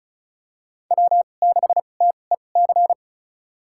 Morse code, heard as a keyed tone, about 36 wpm, spaced as for 25 wpm, 700 Hz. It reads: W6TEC
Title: USA Call Signs - 5 Characters - 2x Character Spacing - 35wpm